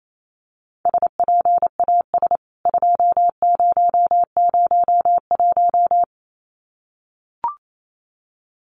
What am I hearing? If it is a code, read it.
Morse code, heard as a keyed tone, about 28 wpm, 700 Hz. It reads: SPAS 2001
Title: CWT Exchange 28wpm